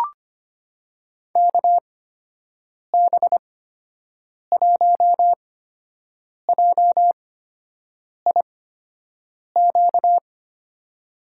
Morse code, heard as a keyed tone, about 25 wpm, spaced as for 7 wpm, 700 Hz. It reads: KB1JIQ
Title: USA Call Signs - 8x Character Spacing - 25wpm